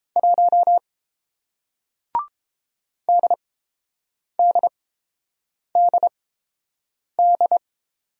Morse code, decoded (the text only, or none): 1 DDDD